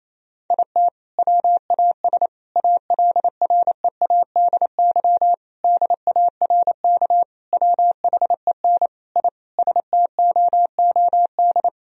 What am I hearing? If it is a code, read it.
Morse code, heard as a keyed tone, about 27 wpm, 700 Hz. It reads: IT WAS ALREADY DARK WHEN I STOOD
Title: Sentences from Top 1000 Words - Review Part 2 - 28wpm